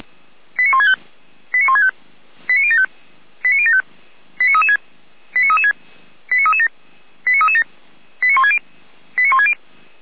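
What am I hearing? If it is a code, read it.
Five-tone selective call (ZVEI), heard as quick runs of five tones, tones of about 70 ms. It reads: 89165 89165 89075 89075 89336 89336 89337 89337 89160 89160
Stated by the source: ZVEI five-tone selective call received by SDR, archived by the Feuerwehr Orlamünde fire brigade.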